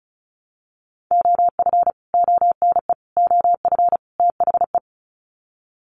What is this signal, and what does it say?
Morse code, heard as a keyed tone, about 34 wpm, 700 Hz. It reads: OF ONE OF THE
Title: Sets of 4 Words 35wpm